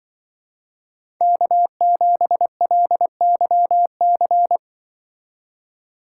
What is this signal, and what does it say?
Morse code, heard as a keyed tone, about 24 wpm, 700 Hz. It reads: K7LYC
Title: Speed Racing - USA 1x3 Call Signs 20wpm